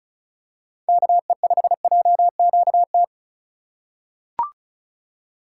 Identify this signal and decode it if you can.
Morse code, heard as a keyed tone, about 35 wpm, 700 Hz. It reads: KE5JQT
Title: USA Call Signs - 6 Characters - 3x Character Spacing - 35wpm